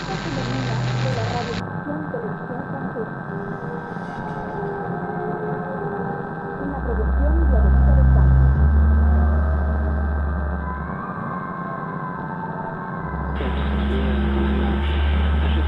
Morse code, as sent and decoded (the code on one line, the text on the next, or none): none